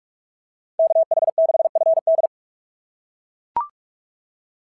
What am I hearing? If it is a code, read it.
Morse code, heard as a keyed tone, about 44 wpm, 690 Hz. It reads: KH6FD
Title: SOTA QSO Practice 45wpm